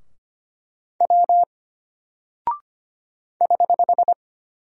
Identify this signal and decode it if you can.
Morse code, heard as a keyed tone, about 25 wpm, 700 Hz. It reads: W <HH>
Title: Single Character (Letter:Number:Punctuation:Prosign) Alt 25wpm